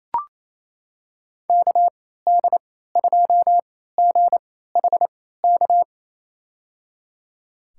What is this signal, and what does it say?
Morse code, heard as a keyed tone, about 28 wpm, spaced as for 16 wpm, 700 Hz. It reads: KD2GHK
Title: USA Call Signs - 6 Characters - 3x Character Spacing - 28wpm